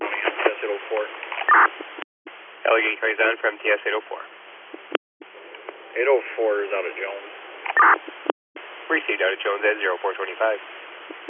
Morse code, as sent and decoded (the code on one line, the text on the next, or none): none